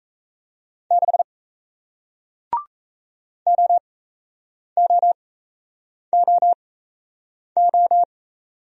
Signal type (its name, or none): Morse code